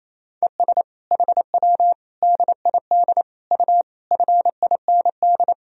Morse code, heard as a keyed tone, about 28 wpm, 700 Hz. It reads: ES HW DID U FIND
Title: Sentences from Top 200 Words and Abbreviations 28wpm